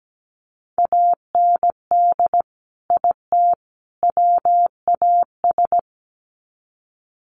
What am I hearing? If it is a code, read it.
Morse code, heard as a keyed tone, about 17 wpm, 700 Hz. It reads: AND IT WAS
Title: Sets of 3 Words 17wpm